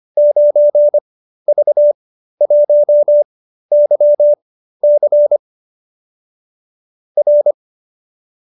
Morse code, heard as a keyed tone, about 25 wpm, spaced as for 13 wpm, 570 Hz. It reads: 9V1YC R